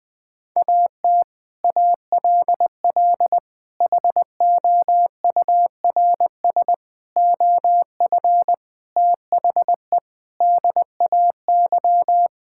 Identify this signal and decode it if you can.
Morse code, heard as a keyed tone, about 20 wpm, 700 Hz. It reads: AT ALL HOURS OF THE DAY
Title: Sets of 8 Words - 20wpm